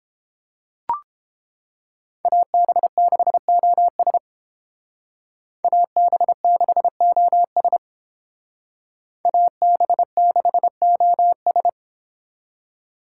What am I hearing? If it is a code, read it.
Morse code, heard as a keyed tone, about 29 wpm, 700 Hz. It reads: AB6OS AB6OS AB6OS